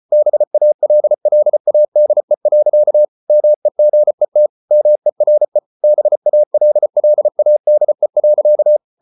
Morse code, heard as a keyed tone, about 34 wpm, 600 Hz. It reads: BALLADE. MEGET MERE BALLADE.
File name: Ferie QRQ 34wpm